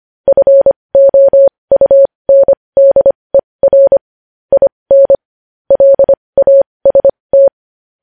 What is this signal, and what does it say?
Morse code, heard as a keyed tone, about 25 wpm, spaced as for 20 wpm, 550 Hz. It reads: FOUNDER IN LAST
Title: News Headlines at 20